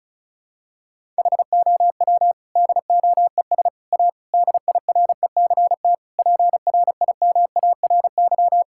Morse code, heard as a keyed tone, about 36 wpm, 700 Hz. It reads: HOW DOES A DIRECT PRIMARY
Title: Sentences from Top 1000 Words - Review Part 5 - 35wpm